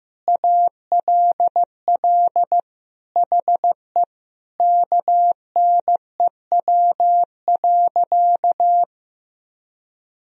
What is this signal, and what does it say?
Morse code, heard as a keyed tone, about 15 wpm, 700 Hz. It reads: ALL HE KNEW.
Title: Sentences from Top 400 Words and Abbreviations 15wpm